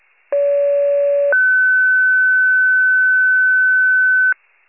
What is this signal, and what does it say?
Two-tone page, heard as a one-second tone, then a three-second tone, about 570 then 1500 Hz. It is heard from 0.3 to 4.3 s.
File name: wcalert